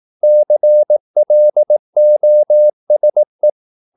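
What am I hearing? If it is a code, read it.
Morse code, heard as a keyed tone, about 18 wpm, 600 Hz. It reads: CLOSE
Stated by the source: Morse practice at 18 wpm, English text, from CW med Gnister.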